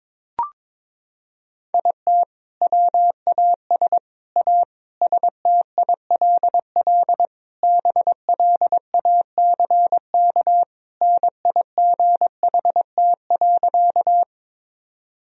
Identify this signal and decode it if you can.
Morse code, heard as a keyed tone, about 22 wpm, 700 Hz. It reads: IT WAS A STILL BLACK NIGHT.
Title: Sentences from Top 1000 Words - Review Part 2 - 22wpm